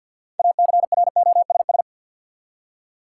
Morse code, heard as a keyed tone, about 53 wpm, 700 Hz. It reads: ACROSS
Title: Six Letter Words - 50wpm